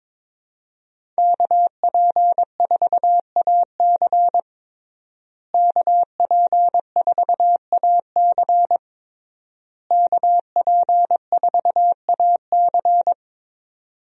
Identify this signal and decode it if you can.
Morse code, heard as a keyed tone, about 21 wpm, 700 Hz. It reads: KP4AC KP4AC KP4AC
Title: USA Call Signs - 5 Characters - Repeated 3x - 22wpm